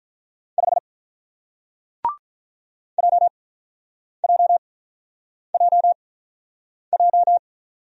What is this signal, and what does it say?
Morse code, heard as a keyed tone, about 48 wpm, spaced as for 9 wpm, 690 Hz. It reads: 5 JJJJ